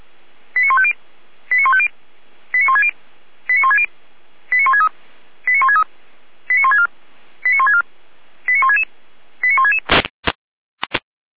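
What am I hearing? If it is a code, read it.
Five-tone selective call (ZVEI), heard as quick runs of five tones, tones of about 70 ms. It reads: 89150 89150 89160 89160 89163 89163 89164 89164 89166 89166
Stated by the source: ZVEI five-tone selective call received on SDR.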